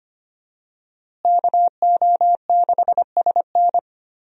Morse code, heard as a keyed tone, about 25 wpm, 700 Hz. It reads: KO6SN
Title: USA Call Signs - 5 Characters - Repeated 3x - 25wpm